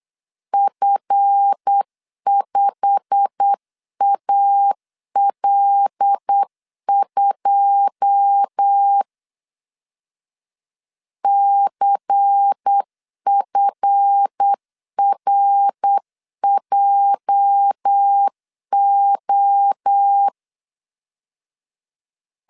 Morse code, heard as a keyed tone, about 8 wpm, 800 Hz. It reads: F5AL2 CFRJO
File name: LEZIONI / 9 28 8wpm 3512